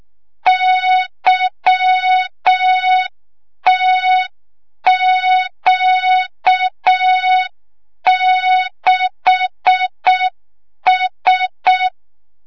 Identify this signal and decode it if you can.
Morse code, heard as a keyed tone, about 6 wpm, 3700 Hz. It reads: YTQ6S
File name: Morze-HA2MN / hang / 25 GYAKORLAT